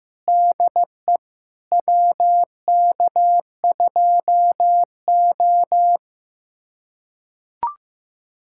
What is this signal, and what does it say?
Morse code, heard as a keyed tone, about 15 wpm, 700 Hz. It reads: DE WK2O
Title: USA Callsign DE Callsign - 4 Characters - 15wpm